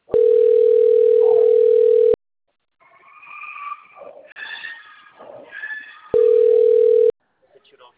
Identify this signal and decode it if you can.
Ringback tone (North America), 440 + 480 Hz.